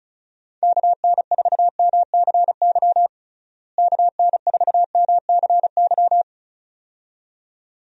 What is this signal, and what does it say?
Morse code, heard as a keyed tone, about 34 wpm, 700 Hz. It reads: KN4MCY KN4MCY
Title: USA Call Signs - 6 Characters - Repeated 2x - 35wpm